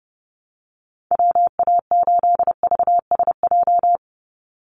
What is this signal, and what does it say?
Morse code, heard as a keyed tone, about 30 wpm, 700 Hz. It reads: WA8VSJ
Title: Call Signs - 30wpm